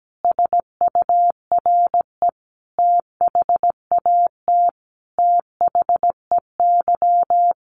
Morse code, heard as a keyed tone, about 17 wpm, 700 Hz. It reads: SURE THAT THEY
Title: Sets of 5 Words Encore2 17wpm